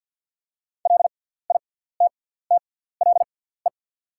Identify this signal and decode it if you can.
Morse code, heard as a keyed tone, about 48 wpm, spaced as for 17 wpm, 690 Hz. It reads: LITTLE
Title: 6x Character Spacing - Top 500 Words - 50wpm